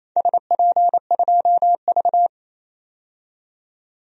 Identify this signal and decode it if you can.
Morse code, heard as a keyed tone, about 28 wpm, 700 Hz. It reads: SP2V
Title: DXCC-100 Call Signs - 4 Characters - 28wpm